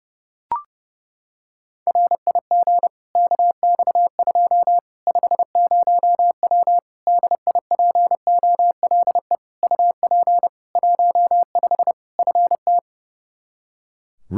Morse code, heard as a keyed tone, about 29 wpm, 700 Hz. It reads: RIG KX2 50W DIPOLE UP 15 FT
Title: K1USN SST QSO Practice 30wpm